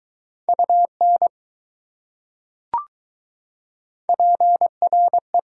Morse code, heard as a keyed tone, about 20 wpm, 700 Hz. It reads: UN PRE